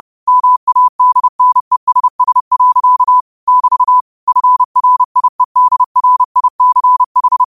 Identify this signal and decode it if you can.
Morse code, heard as a keyed tone, about 30 wpm, 1000 Hz. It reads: MADNESS. <BT> FRIEDRICH